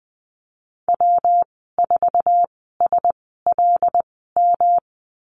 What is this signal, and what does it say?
Morse code, heard as a keyed tone, about 20 wpm, spaced as for 15 wpm, 700 Hz. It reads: W4SLM
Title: 154 Lesson 25g - Letter p - Call Signs 20wpm